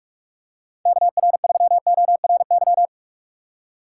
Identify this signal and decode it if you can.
Morse code, heard as a keyed tone, about 46 wpm, 700 Hz. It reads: KR3ORY
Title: USA Call Signs - 6 Characters - 4x Character Spacing - 45wpm